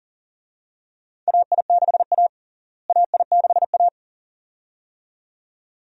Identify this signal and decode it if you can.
Morse code, heard as a keyed tone, about 40 wpm, 700 Hz. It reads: AI6A AI6A
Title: USA Call Signs - 4 Characters - Repeated 2x - 40wpm